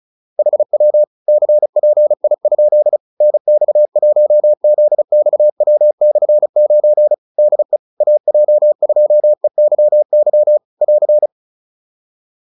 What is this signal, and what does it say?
Morse code, heard as a keyed tone, about 34 wpm, 600 Hz. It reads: HW CPI? NX1ZXW/9 DE AJ2EYY <AR>